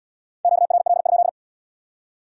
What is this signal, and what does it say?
Morse code, heard as a keyed tone, about 77 wpm, 700 Hz. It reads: QRL?